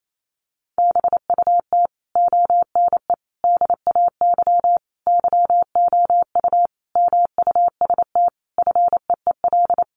Morse code, heard as a keyed tone, about 28 wpm, 700 Hz. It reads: BUT ONE DAY YOU MUST FEEL
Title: Sentences from Top 400 Words 28wpm